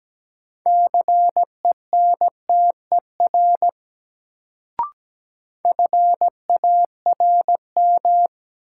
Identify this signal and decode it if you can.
Morse code, heard as a keyed tone, about 17 wpm, 700 Hz. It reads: CENTER FARM
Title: Sentences from Top 1000 Words - Review Part 2 - New Word Drill - 17wpm